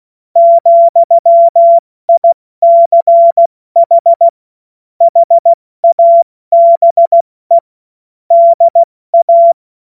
Morse code, heard as a keyed tone, about 16 wpm, 680 Hz. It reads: ,ICH HABE DA
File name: Witz-22-Juli-2025-70-F80